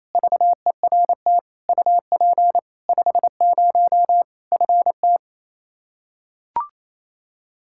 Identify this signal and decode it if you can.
Morse code, heard as a keyed tone, about 28 wpm, 700 Hz. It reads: VERT UP 50 FT
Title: Practice QSO - Exchange 2 - Info - 28wpm